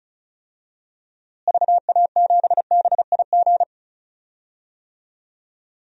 Morse code, heard as a keyed tone, about 36 wpm, 700 Hz. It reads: VA7BIG
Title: DXCC-100 Call Signs - 6 Characters - Repeated 3x - 35wpm